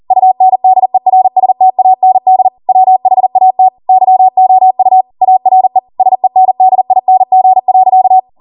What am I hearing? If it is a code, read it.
Morse code, heard as a keyed tone, about 40 wpm, 750 Hz. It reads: UNDERSTAND WHAT YOU ARE SENDING.